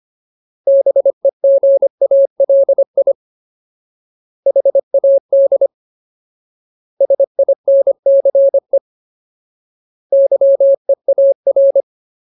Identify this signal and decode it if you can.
Morse code, heard as a keyed tone, about 25 wpm, 550 Hz. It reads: BEGALI HAD SINCE YEAR
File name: Ultimate Word List / 25WPM Ultimate